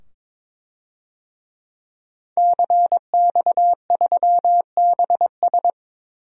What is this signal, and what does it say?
Morse code, heard as a keyed tone, about 22 wpm, 700 Hz. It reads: CX3BS
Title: DXCC-100 Call Signs - 5 Characters - 22wpm